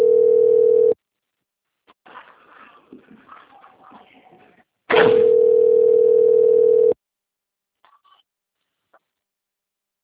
Ringback tone (North America), 440 + 480 Hz.